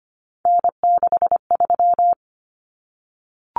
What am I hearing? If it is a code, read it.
Morse code, heard as a keyed tone, about 25 wpm, 700 Hz. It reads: N63
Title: Common Cut Numbers - 3-Digit 25wpm